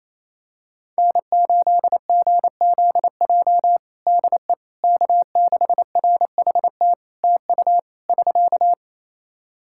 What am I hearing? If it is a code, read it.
Morse code, heard as a keyed tone, about 28 wpm, 700 Hz. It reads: N8GZJ DE K6RHT TU <SK>
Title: Practice QSO - Exchange 4 - Ending - 28wpm